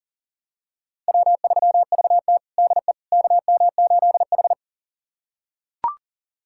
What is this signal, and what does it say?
Morse code, heard as a keyed tone, about 40 wpm, 700 Hz. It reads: W3VT DE KM8H
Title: USA Callsign DE Callsign - 4 Characters - 40wpm